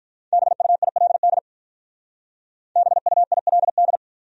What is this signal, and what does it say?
Morse code, heard as a keyed tone, about 46 wpm, 700 Hz. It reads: BUILD BUILD